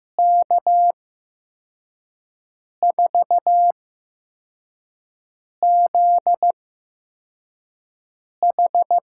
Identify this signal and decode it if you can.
Morse code, heard as a keyed tone, about 15 wpm, spaced as for 4 wpm, 700 Hz. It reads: K4ZH
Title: USA Call Signs - 5 Characters - 8x Character Spacing - 15wpm